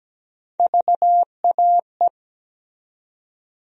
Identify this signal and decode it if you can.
Morse code, heard as a keyed tone, about 17 wpm, 700 Hz. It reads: VAE
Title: USA Call Sign Postfixes - 3 Letters 17wpm